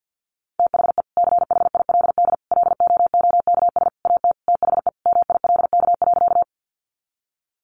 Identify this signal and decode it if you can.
Morse code, heard as a keyed tone, about 50 wpm, 700 Hz. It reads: THE CHILD LOOKS AT THE MILK.
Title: Additional Easy Phrases 50wpm